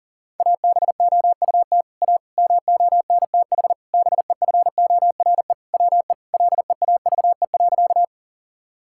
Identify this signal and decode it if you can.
Morse code, heard as a keyed tone, about 40 wpm, 700 Hz. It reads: ABOUT A MONTH BEFORE WE LEAVE.
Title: Sentences from Top 1000 Words - Review Part 5 - 40wpm